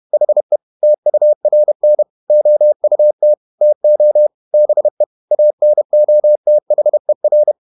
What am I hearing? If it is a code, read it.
Morse code, heard as a keyed tone, about 31 wpm, 600 Hz. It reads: HE TURN OUT TO BE ANOTHER
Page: Never 31wpm | CW med Gnister